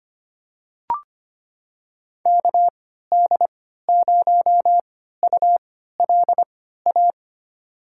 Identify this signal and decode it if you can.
Morse code, heard as a keyed tone, about 25 wpm, spaced as for 14 wpm, 700 Hz. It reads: KD0ULA
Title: USA Call Signs - 6 Characters - 3x Character Spacing - 25wpm